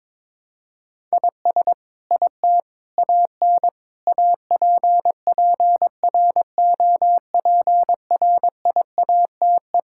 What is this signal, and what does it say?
Morse code, heard as a keyed tone, about 22 wpm, 700 Hz. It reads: IS IT AN APPROPRIATE
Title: Sentences from Top 1000 Words - Review Part 5 - 22wpm